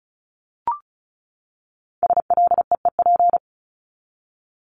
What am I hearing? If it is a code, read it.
Morse code, heard as a keyed tone, about 34 wpm, 700 Hz. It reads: SLEEP
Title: Top 500 Words 35wpm